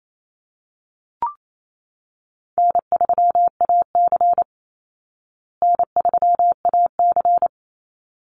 Morse code, heard as a keyed tone, about 28 wpm, 700 Hz. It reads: N3AC N3AC
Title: Easy Call Signs Repeated 2x - 28wpm